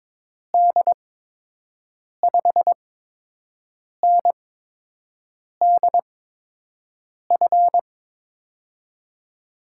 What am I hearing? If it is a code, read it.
Morse code, heard as a keyed tone, about 22 wpm, spaced as for 6 wpm, 700 Hz. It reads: D5NDF
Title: USA Call Signs - 6 Characters - 8x Character Spacing - 22wpm